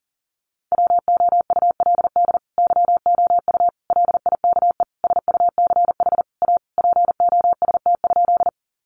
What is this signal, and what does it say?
Morse code, heard as a keyed tone, about 40 wpm, 700 Hz. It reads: WOULD YOU LIKE SUCH A POST?
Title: Sentences from Top 1000 Words 40wpm